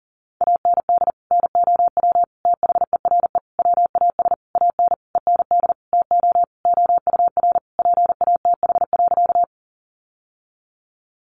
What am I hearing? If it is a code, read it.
Morse code, heard as a keyed tone, about 40 wpm, 700 Hz. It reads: AND NOW THERE WAS AN END TO OUR PATH.
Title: Sentences from Top 1400 Words 40wpm